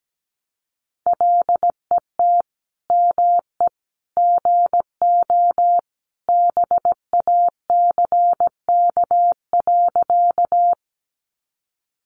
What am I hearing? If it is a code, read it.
Morse code, heard as a keyed tone, about 17 wpm, 700 Hz. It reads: LET ME GO BACK.
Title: Sentences from Top 300 Words 17wpm